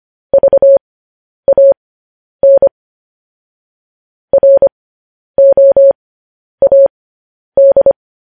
Morse code, heard as a keyed tone, about 25 wpm, spaced as for 10 wpm, 550 Hz. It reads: VAN ROAD
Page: news-headlines-in-morse-code-at-10-wpm